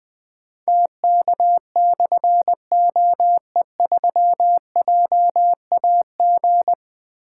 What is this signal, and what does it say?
Morse code, heard as a keyed tone, about 20 wpm, 700 Hz. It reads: TK/OE3JAG
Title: Difficult Call Signs 20wpm